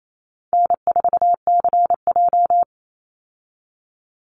Morse code, heard as a keyed tone, about 28 wpm, 700 Hz. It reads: N4CJ
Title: Easy Call Signs Repeated 3x - 28wpm